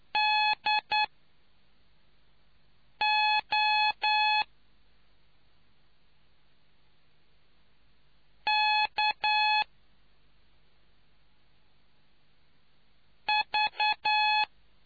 Morse code, heard as a keyed tone, about 9 wpm, spaced as for 4 wpm, 850 Hz. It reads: DO K V